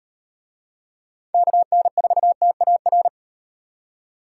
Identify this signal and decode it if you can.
Morse code, heard as a keyed tone, about 35 wpm, 700 Hz. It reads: KN4TAR